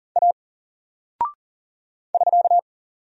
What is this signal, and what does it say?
Morse code, heard as a keyed tone, about 39 wpm, 700 Hz. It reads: A <SK>
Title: Single Character (Letter:Number:Punctuation:Prosign) 40wpm